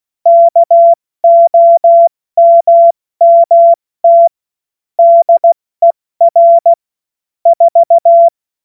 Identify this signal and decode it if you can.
Morse code, heard as a keyed tone, about 16 wpm, 680 Hz. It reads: KOMMT DER 4
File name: Witz-08-Juli-2025-70-F80